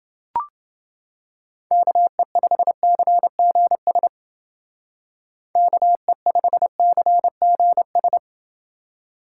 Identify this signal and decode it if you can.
Morse code, heard as a keyed tone, about 30 wpm, 700 Hz. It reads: KE5CGS KE5CGS